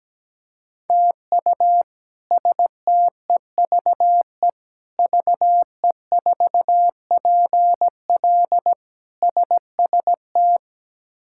Morse code, heard as a keyed tone, about 17 wpm, 700 Hz. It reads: TU STEVE VE4PL SST
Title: K1USN SST QSO Practice - 17wpm